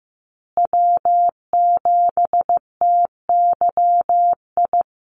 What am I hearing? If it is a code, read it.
Morse code, heard as a keyed tone, about 15 wpm, 700 Hz. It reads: W7TYI
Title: USA Call Signs Repeated 2x 15wpm